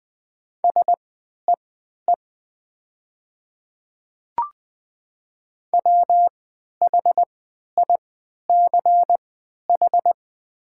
Morse code, heard as a keyed tone, about 20 wpm, spaced as for 11 wpm, 700 Hz. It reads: SEE WHICH
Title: 3x Character Spacing - Top 200 Words - 20wpm